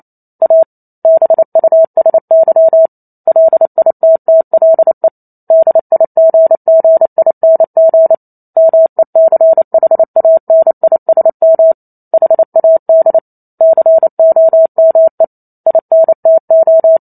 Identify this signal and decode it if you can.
Morse code, heard as a keyed tone, about 28 wpm, 650 Hz. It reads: A BUSY LITTLE DIGGING MECHANISM HAD COME INTO